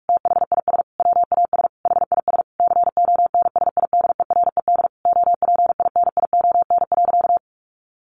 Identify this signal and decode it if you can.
Morse code, heard as a keyed tone, about 44 wpm, 700 Hz. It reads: THIS WAS HIS CONSIDERED OPINION.
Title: A Sound Of Thunder 45wpm